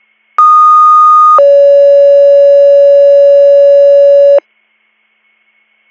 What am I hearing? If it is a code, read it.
Two-tone page, heard as a one-second tone, then a three-second tone, about 1230 then 560 Hz. It is heard from 0.4 to 4.4 s.